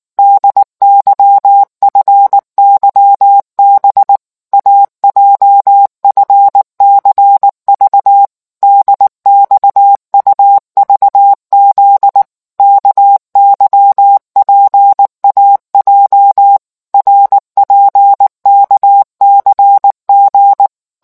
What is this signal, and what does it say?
Morse code, heard as a keyed tone, about 19 wpm, 800 Hz. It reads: DYFYB AJFCV DXUVZ KYPAJ RPKCG